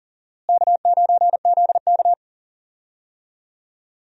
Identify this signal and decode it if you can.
Morse code, heard as a keyed tone, about 39 wpm, 700 Hz. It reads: K9ZK
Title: USA Call Signs - 4 Characters - Repeated 2x - 40wpm